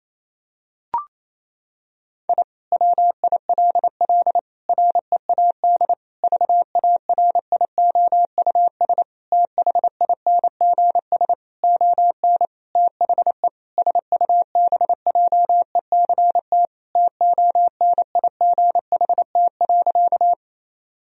Morse code, heard as a keyed tone, about 28 wpm, 700 Hz. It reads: I WILL READ VARIOUS THINGS ON THE SUBJECT TONIGHT.
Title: Sentences from Top 1000 Words - Review Part 4 - 28wpm